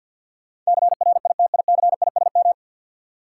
Morse code, heard as a keyed tone, about 52 wpm, 700 Hz. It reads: CRITICISM
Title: Sentences from Top 1300 Words - New Word Drill Alt - 50wpm